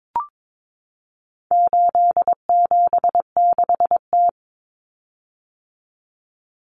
Morse code, heard as a keyed tone, about 22 wpm, 700 Hz. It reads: 876T
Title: Common Cut Numbers - 4-Digit 22wpm